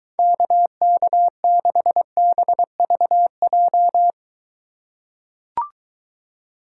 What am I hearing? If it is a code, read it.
Morse code, heard as a keyed tone, about 23 wpm, 700 Hz. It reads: KK6BVJ